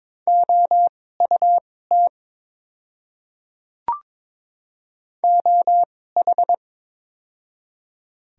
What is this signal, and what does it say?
Morse code, heard as a keyed tone, about 21 wpm, spaced as for 16 wpm, 700 Hz. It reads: OUT OH
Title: 2x Character Spacing - Top 500 Words - 22wpm